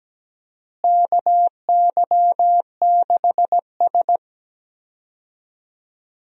Morse code, heard as a keyed tone, about 17 wpm, 700 Hz. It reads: KY6S
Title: USA Call Signs - 4 Characters - Repeated 2x - 17wpm